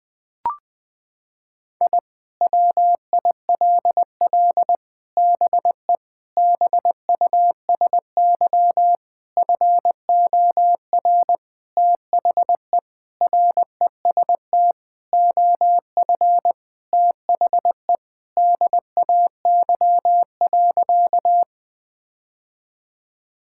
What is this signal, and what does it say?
Morse code, heard as a keyed tone, about 21 wpm, 700 Hz. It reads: I WILL BE BUSY FOR THE REST OF THE DAY.
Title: Sentences with 5-Word Phrases - 20wpm